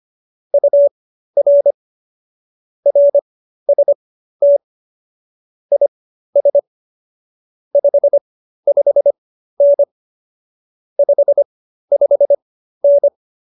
Morse code, heard as a keyed tone, about 25 wpm, spaced as for 13 wpm, 570 Hz. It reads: UR RST IS 55N 55N